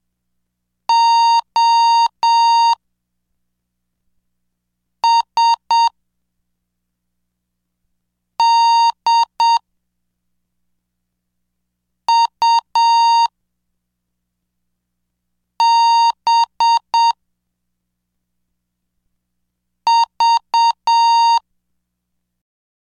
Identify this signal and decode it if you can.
Morse code, heard as a keyed tone, about 7 wpm, 930 Hz. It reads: O S D U B V
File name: cursos / CDestudo / 07 Track 7